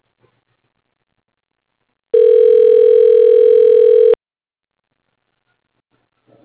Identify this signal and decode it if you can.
Ringback tone (North America), 440 + 480 Hz.
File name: recordings / REfc03b00d6d988f9fe1716ba49b64bf47